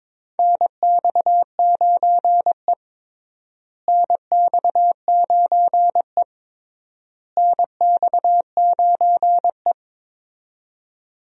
Morse code, heard as a keyed tone, about 22 wpm, 700 Hz. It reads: NX9E NX9E NX9E
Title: USA Call Signs - 4 Characters - Repeated 3x - 22wpm